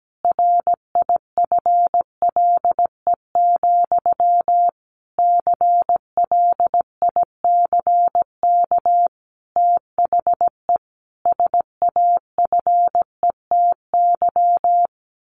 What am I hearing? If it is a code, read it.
Morse code, heard as a keyed tone, about 17 wpm, 700 Hz. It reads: RIFLE, CLICK THE SAFETY
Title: A Sound Of Thunder 17wpm